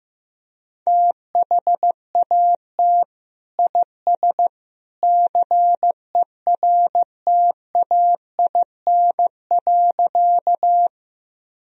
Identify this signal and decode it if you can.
Morse code, heard as a keyed tone, about 14 wpm, 700 Hz. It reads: THAT IS CERTAIN.
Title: Sentences from Top 1000 Words - Review Part 2 - 15wpm